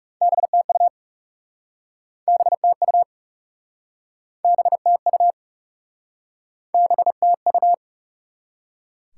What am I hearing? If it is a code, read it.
Morse code, heard as a keyed tone, about 40 wpm, 700 Hz. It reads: BTU BTU BTU BTU